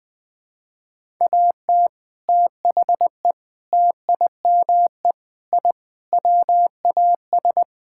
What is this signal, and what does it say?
Morse code, heard as a keyed tone, about 20 wpm, 700 Hz. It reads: AT THE TIME I WAS
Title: Sentences with 3-Word Phrases - Spoken Beginning - 20wpm